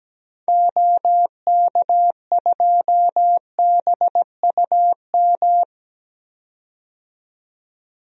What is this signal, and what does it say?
Morse code, heard as a keyed tone, about 17 wpm, 700 Hz. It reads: OK2BUM
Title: DXCC-100 Call Signs - 6 Characters - 17wpm